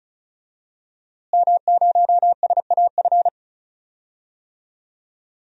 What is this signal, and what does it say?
Morse code, heard as a keyed tone, about 34 wpm, 700 Hz. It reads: M0SAF
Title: DXCC-100 Call Signs - 5 Characters - Repeated 2x - 35wpm